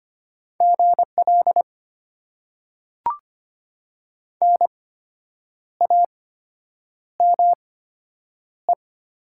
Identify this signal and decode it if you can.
Morse code, heard as a keyed tone, about 25 wpm, spaced as for 8 wpm, 700 Hz. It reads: GL NAME
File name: Top 90 QSO Elements - 8x Character Spacing - 25wpm